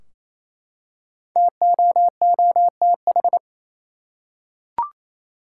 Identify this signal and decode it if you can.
Morse code, heard as a keyed tone, about 28 wpm, 700 Hz. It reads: TOOTH
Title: Five Letter Words - 28wpm